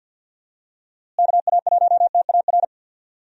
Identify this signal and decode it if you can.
Morse code, heard as a keyed tone, about 50 wpm, 700 Hz. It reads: KA1TAR
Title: USA Call Signs - 6 Characters - Repeated 2x - 50wpm